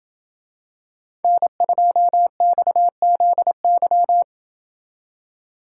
Morse code, heard as a keyed tone, about 27 wpm, 700 Hz. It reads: N2XZY